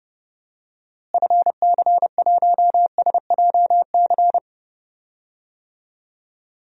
Morse code, heard as a keyed tone, about 30 wpm, 700 Hz. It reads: FC1SJC
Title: DXCC-100 Call Signs - 6 Characters - Repeated 3x - 30wpm